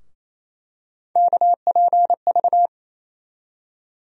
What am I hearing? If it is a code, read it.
Morse code, heard as a keyed tone, about 28 wpm, 700 Hz. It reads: KPV
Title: USA Call Sign Postfixes - 3 Letters 28wpm